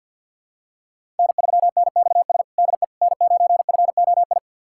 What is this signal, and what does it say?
Morse code, heard as a keyed tone, about 48 wpm, 700 Hz. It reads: N3NXS DE N9FOI
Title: USA Callsign DE Callsign - 5 Characters - 50wpm